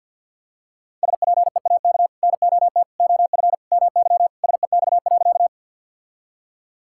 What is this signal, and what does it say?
Morse code, heard as a keyed tone, about 50 wpm, 700 Hz. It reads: SPEAK NOT OF MY SEX.
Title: Sentences from Top 1600 Words 50wpm